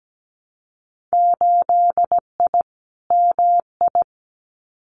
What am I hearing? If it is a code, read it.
Morse code, heard as a keyed tone, about 17 wpm, 700 Hz. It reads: 8I MI
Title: Winter Field Day Exchanges Encore 17wpm